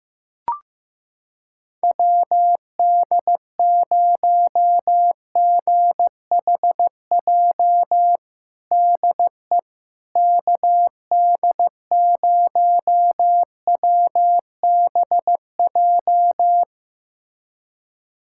Morse code, heard as a keyed tone, about 15 wpm, 700 Hz. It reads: WD0GHJ DE KD0WBJ
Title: USA Callsign DE Callsign - 6 Characters - 15wpm